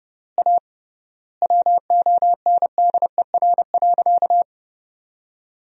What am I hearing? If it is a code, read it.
Morse code, heard as a keyed tone, about 30 wpm, 700 Hz. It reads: A WONDER.
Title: 3x Word-Spacing - Sentences from Top 500 Words - 30wpm